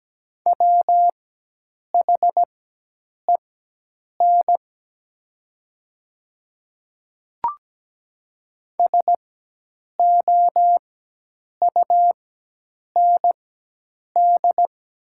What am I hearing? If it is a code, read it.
Morse code, heard as a keyed tone, about 17 wpm, spaced as for 8 wpm, 700 Hz. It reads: WHEN SOUND
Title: 4x Character Spacing - Top 100 Words - 17wpm